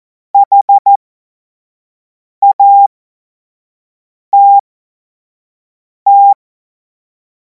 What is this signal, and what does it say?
Morse code, heard as a keyed tone, about 14 wpm, spaced as for 5 wpm, 800 Hz. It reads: HATT
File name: Morsekurs / Witze / Witz 03. Februar 2026 25 F70